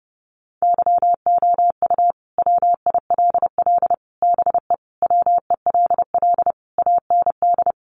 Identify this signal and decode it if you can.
Morse code, heard as a keyed tone, about 30 wpm, 700 Hz. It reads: YOU WILL BE WELL AND
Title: Sentences from Top 1000 Words 30wpm